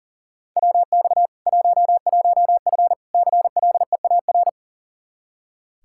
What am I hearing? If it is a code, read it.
Morse code, heard as a keyed tone, about 40 wpm, 700 Hz. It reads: WX 11F CLEAR
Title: Practice QSO - Exchange 2 - Info - 40wpm